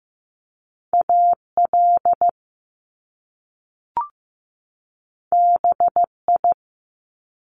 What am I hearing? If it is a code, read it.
Morse code, heard as a keyed tone, about 15 wpm, 700 Hz. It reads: AL BI
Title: Top Two Letter Combinations in Words 15wpm